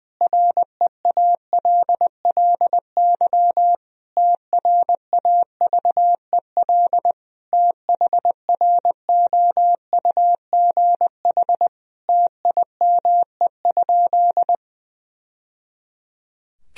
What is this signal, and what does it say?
Morse code, heard as a keyed tone, about 20 wpm, 700 Hz. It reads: REALLY TRAVEL THROUGH TIME?
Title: Sentences from Top 1000 Words - Review Part 2 - 20wpm